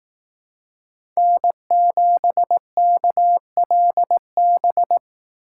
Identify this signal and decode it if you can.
Morse code, heard as a keyed tone, about 18 wpm, 700 Hz. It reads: N7KLB
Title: Speed Racing - USA 1x3 Call Signs 15wpm